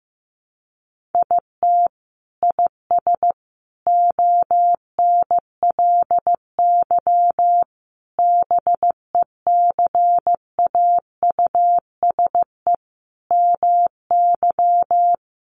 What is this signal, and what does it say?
Morse code, heard as a keyed tone, about 15 wpm, 700 Hz. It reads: IT IS ONLY BECAUSE MY
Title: Sentences from Top 1400 Words 15wpm